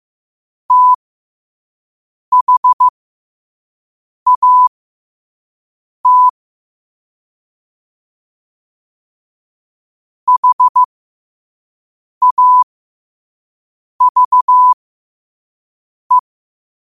Morse code, heard as a keyed tone, about 15 wpm, spaced as for 5 wpm, 1000 Hz. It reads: THAT HAVE
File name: qotdfiles / 5wpm